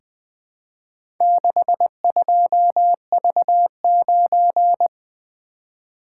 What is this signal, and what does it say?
Morse code, heard as a keyed tone, about 20 wpm, 700 Hz. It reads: 62V9
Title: Four Letter-Number Combinations - 20wpm